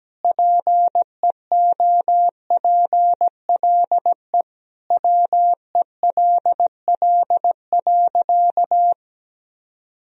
Morse code, heard as a keyed tone, about 17 wpm, 700 Hz. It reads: PEOPLE WELL.
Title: Sentences from Top 1000 Words - Review Part 2 - 17wpm